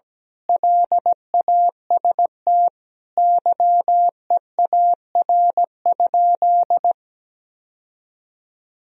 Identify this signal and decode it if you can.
Morse code, heard as a keyed tone, about 17 wpm, 700 Hz. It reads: LAST YEAR?
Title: Sentences from Top 1000 Words - Review Part 4 - 17wpm